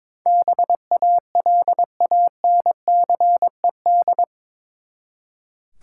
Morse code, heard as a keyed tone, about 22 wpm, 700 Hz. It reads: BALANCED
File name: practice / 22wpm / Eight Letter Words - 22wpm